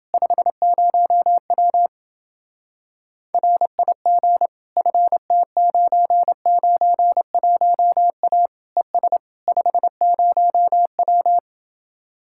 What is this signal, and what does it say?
Morse code, heard as a keyed tone, about 29 wpm, 700 Hz. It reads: 50W RIG FT991A ES 50W